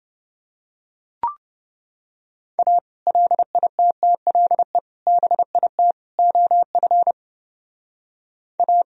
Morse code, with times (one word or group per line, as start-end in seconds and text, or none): 2.6-2.8 A
3.1-4.8 LITTLE
5.1-5.9 BIT
6.2-7.1 OF
8.6-8.8 A